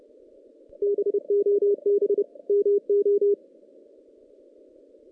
Morse code, heard as a keyed tone, about 28 wpm, 380 Hz. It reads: BOB MO
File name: 2016 JAN CW NAQP